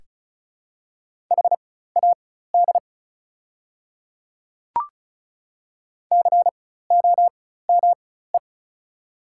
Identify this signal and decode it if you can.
Morse code, heard as a keyed tone, about 36 wpm, spaced as for 16 wpm, 690 Hz. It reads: HAD COME
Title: 4x Character Spacing - Top 100 Words - 35wpm